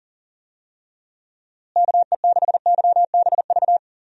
Morse code, heard as a keyed tone, about 40 wpm, 700 Hz. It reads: KE6YBV
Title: USA Call Signs - 6 Characters - Repeated 3x - 40wpm